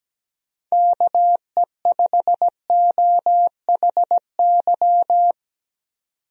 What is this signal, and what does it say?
Morse code, heard as a keyed tone, about 17 wpm, 700 Hz. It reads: KE5OHY
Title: USA Call Signs - 6 Characters - 17wpm